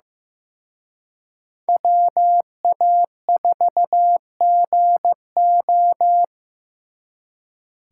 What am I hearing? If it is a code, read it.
Morse code, heard as a keyed tone, about 18 wpm, 700 Hz. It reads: WA4GO
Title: Speed Racing - USA 2x2 Call Signs 15wpm